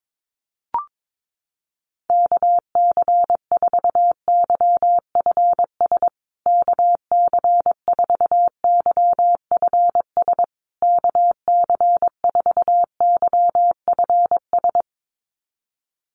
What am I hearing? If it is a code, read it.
Morse code, heard as a keyed tone, about 22 wpm, 700 Hz. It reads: KC4YFS KC4YFS KC4YFS
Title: USA Call Signs Repeated 3x 22wpm